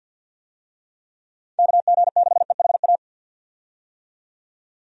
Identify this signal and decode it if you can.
Morse code, heard as a keyed tone, about 51 wpm, 700 Hz. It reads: KG6EHA
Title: USA Call Signs - 6 Characters - 50wpm